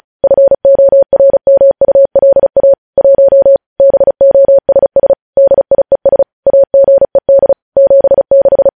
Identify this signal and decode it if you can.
Morse code, heard as a keyed tone, about 36 wpm, 550 Hz. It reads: FORMULA 1 BOSS DIES AGED 76
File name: podcast / News headlines-1x-35x35